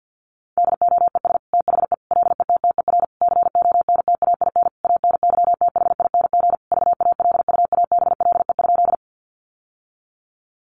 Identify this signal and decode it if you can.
Morse code, heard as a keyed tone, about 50 wpm, 700 Hz. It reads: DOES THE LETTER CONTAIN ANYTHING VALUABLE?
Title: Sentences from Top 1100 Words 50wpm